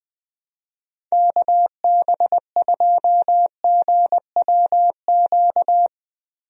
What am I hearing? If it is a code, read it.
Morse code, heard as a keyed tone, about 21 wpm, 700 Hz. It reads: KB2GWQ
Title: USA Call Signs - 6 Characters - Repeated 3x - 20wpm